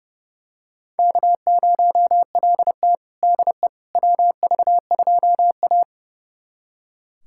Morse code, heard as a keyed tone, about 32 wpm, 700 Hz. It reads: K0LT DE WV2A
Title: USA Callsign DE Callsign - 4 Characters - 30wpm